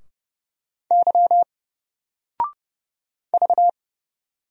Morse code, heard as a keyed tone, about 30 wpm, 700 Hz. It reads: Y V
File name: Single-Letter-Number-30wpm